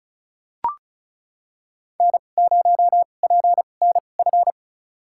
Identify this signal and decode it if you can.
Morse code, heard as a keyed tone, about 34 wpm, spaced as for 25 wpm, 700 Hz. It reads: N0PNF
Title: USA Call Signs - 5 Characters - 2x Character Spacing - 35wpm